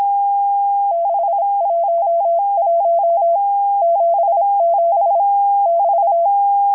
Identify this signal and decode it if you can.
Morse code, heard as a keyed tone, about 26 wpm, 680 Hz. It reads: 611 77 <BT>
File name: HM02- -0437utc-7351khz